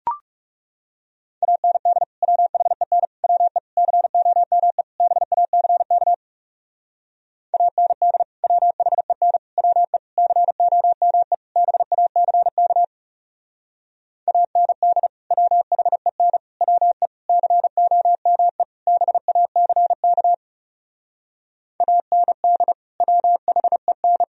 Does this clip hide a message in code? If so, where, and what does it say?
Yes: Morse code from 1.4 to 24.2 s, heard as a keyed tone, about 40 wpm, 700 Hz. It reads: AND WHEN WE COME BACK AND WHEN WE COME BACK AND WHEN WE COME BACK AND WHEN